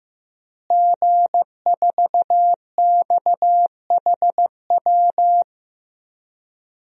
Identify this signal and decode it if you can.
Morse code, heard as a keyed tone, about 15 wpm, 700 Hz. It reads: G4XHW
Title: DXCC-100 Call Signs - 5 Characters - Repeated 2x - 15wpm